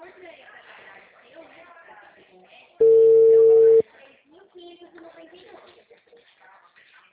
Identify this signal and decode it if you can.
Ringback tone (China, 450 Hz), with speech.
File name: out- -2020- - - .95674